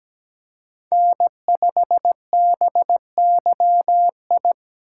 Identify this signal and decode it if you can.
Morse code, heard as a keyed tone, about 17 wpm, 700 Hz. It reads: N5BYI
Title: USA Call Signs - 5 Characters - 2x Character Spacing - 17wpm